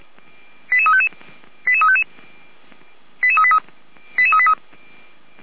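Five-tone selective call (ZVEI), heard as quick runs of five tones, tones of about 70 ms. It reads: 88355 88355 88383 88383
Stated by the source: ZVEI five-tone selective call received on an SDR.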